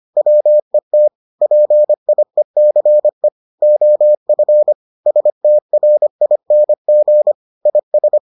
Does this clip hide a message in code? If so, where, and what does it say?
Yes: Morse code from 0.2 to 8.2 s, heard as a keyed tone, about 25 wpm, 600 Hz. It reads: WET PIECE OF STRING IS